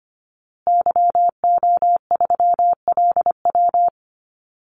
Morse code, heard as a keyed tone, about 25 wpm, 700 Hz. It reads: YO3LW
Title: Call Signs Repeated 2x - 25wpm